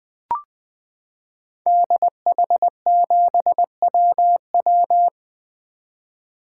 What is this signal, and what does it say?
Morse code, heard as a keyed tone, about 20 wpm, 700 Hz. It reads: DH7WW
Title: 2023 SOTA Activator Call Signs - 20wpm